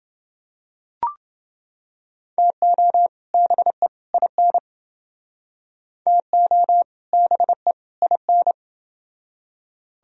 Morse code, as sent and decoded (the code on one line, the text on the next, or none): - --- / -... / .. -. / - --- / -... / .. -.
TO BE IN TO BE IN